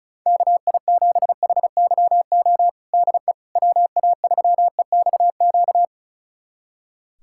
Morse code, heard as a keyed tone, about 34 wpm, 700 Hz. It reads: KI7HYO DE WA3EXQ
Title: USA Callsign DE Callsign - 6 Characters - 35wpm